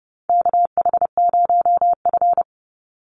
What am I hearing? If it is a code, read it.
Morse code, heard as a keyed tone, about 30 wpm, 700 Hz. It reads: KH0F
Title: USA Call Signs 30wpm